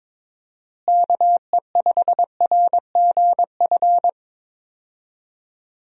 Morse code, heard as a keyed tone, about 22 wpm, 700 Hz. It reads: KE5RGF
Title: USA Call Signs - 6 Characters - Repeated 2x - 22wpm